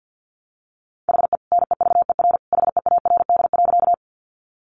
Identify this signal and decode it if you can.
Morse code, heard as a keyed tone, about 50 wpm, 700 Hz. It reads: HE NEVER HEARD.
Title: Sentences from Top 400 Words 50wpm